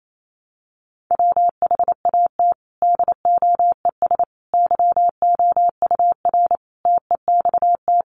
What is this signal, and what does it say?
Morse code, heard as a keyed tone, about 28 wpm, 700 Hz. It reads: WHAT DOES YOUR TEXT
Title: Sentences from Top 1200 Words 28wpm